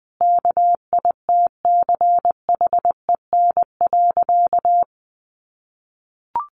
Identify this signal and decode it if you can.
Morse code, heard as a keyed tone, about 20 wpm, 700 Hz. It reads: KITCHEN.
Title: Sentences from Top 1400 Words 20wpm